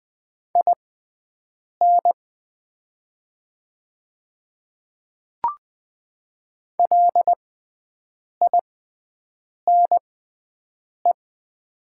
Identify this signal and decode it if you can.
Morse code, heard as a keyed tone, about 20 wpm, spaced as for 7 wpm, 700 Hz. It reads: IN LINE
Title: 6x Character Spacing - Top 200 Words - 20wpm